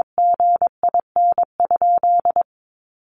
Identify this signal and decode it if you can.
Morse code, heard as a keyed tone, about 22 wpm, 700 Hz. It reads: GIN?
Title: Sentences from Top 1400 Words 22wpm